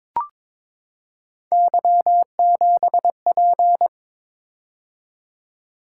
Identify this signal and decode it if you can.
Morse code, heard as a keyed tone, about 22 wpm, 700 Hz. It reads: Y7P
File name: practice / 22wpm / Three Letter-Number Combinations - 22wpm